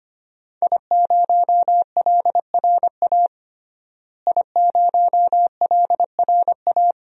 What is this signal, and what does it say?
Morse code, heard as a keyed tone, about 25 wpm, 700 Hz. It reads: I0LRA I0LRA
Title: DXCC-100 Call Signs - 5 Characters - Repeated 3x - 25wpm